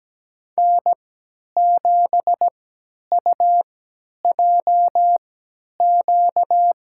Morse code, heard as a keyed tone, about 17 wpm, spaced as for 10 wpm, 700 Hz. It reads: N7UJQ
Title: USA Call Signs - 3x Character Spacing - 17wpm